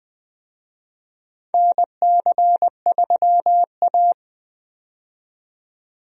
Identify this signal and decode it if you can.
Morse code, heard as a keyed tone, about 20 wpm, 700 Hz. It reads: NC3A
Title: USA Call Signs - 4 Characters - 20wpm